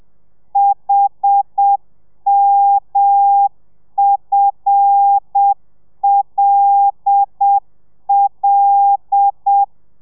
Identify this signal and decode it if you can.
Morse code, heard as a keyed tone, about 7 wpm, 800 Hz. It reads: HMFLL
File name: telegrafia / i2tfj / Lez5 30